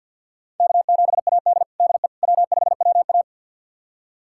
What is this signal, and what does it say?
Morse code, heard as a keyed tone, about 51 wpm, 700 Hz. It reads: K7AD DE W5WA
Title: USA Callsign DE Callsign - 4 Characters - 50wpm